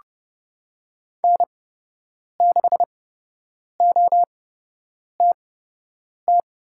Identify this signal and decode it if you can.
Morse code, heard as a keyed tone, about 30 wpm, spaced as for 8 wpm, 700 Hz. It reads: N6OTT
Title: USA Call Signs - 5 Characters - 8x Character Spacing - 30wpm